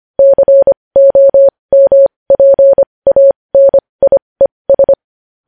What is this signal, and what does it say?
Morse code, heard as a keyed tone, about 25 wpm, spaced as for 20 wpm, 550 Hz. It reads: COMPANIES
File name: podcast / News headlines-1x-25x20